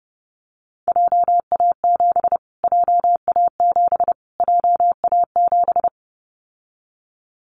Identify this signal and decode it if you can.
Morse code, heard as a keyed tone, about 30 wpm, 700 Hz. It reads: JA7 JA7 JA7
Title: Callsign Prefixes for Top 25 DXCC - Repeated 3x - 30wpm